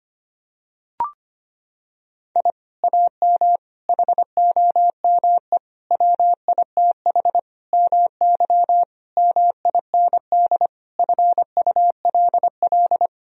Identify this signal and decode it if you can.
Morse code, heard as a keyed tone, about 25 wpm, 700 Hz. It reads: I AM HOME WITH MY MIND FULL
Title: Sentences from Top 1000 Words - Review Part 3 - 25wpm